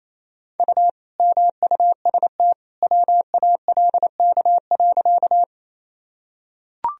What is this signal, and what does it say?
Morse code, heard as a keyed tone, about 27 wpm, 700 Hz. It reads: U MUST WALK.
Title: Sentences from Top 300 Words and Abbreviations 28wpm